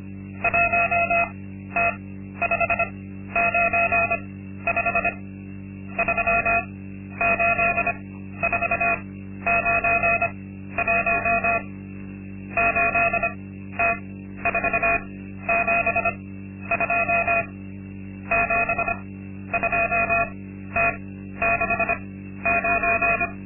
Morse code, heard as a keyed tone, about 25 wpm, spaced as for 12 wpm, 690 Hz. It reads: 1T595 38491 8T472 72T69
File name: Linksmakalnis-2